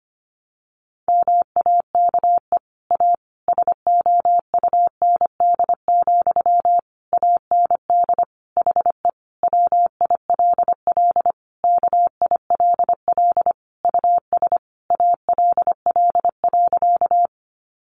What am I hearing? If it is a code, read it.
Morse code, heard as a keyed tone, about 25 wpm, 700 Hz. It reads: MAKE A SOUND, AND HE WILL KILL US ALL.
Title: Sentences from Top 1400 Words 25wpm